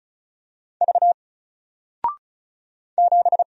Morse code, heard as a keyed tone, about 36 wpm, 700 Hz. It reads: V 7
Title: Single Letter-Number Alt 35wpm